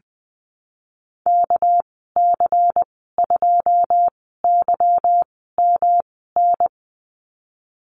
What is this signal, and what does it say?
Morse code, heard as a keyed tone, about 20 wpm, spaced as for 14 wpm, 700 Hz. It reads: KC2YMN
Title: 245 Lesson 39g - Letter z - Call Signs 20wpm